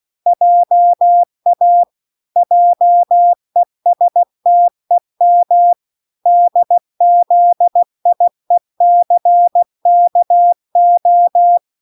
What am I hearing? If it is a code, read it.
Morse code, heard as a keyed tone, about 16 wpm, 700 Hz. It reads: JA JESTEM DZIECKO